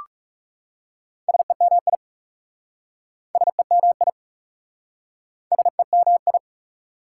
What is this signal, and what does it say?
Morse code, heard as a keyed tone, about 37 wpm, 700 Hz. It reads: SEMI SEMI SEMI
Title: Speed-Racing Prefixes and Suffixes 30wpm